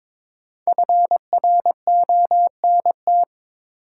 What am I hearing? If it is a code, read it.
Morse code, heard as a keyed tone, about 22 wpm, 700 Hz. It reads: FRONT
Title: Sentences from Top 500 Words - New Word Drill - 22wpm